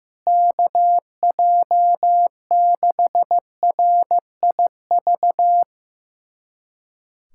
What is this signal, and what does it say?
Morse code, heard as a keyed tone, about 15 wpm, 700 Hz. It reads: KJ6RIV
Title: USA Callsign DE Callsign - 6 Characters - 15wpm